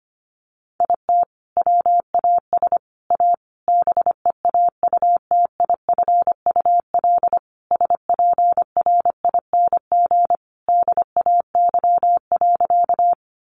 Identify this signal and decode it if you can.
Morse code, heard as a keyed tone, about 25 wpm, 700 Hz. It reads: IT WAS A BEAUTIFUL SPRING DAY.
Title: Sentences from Top 1200 Words 25wpm